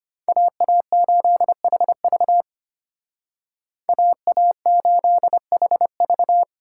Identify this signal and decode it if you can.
Morse code, heard as a keyed tone, about 32 wpm, 700 Hz. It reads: AA8HV AA8HV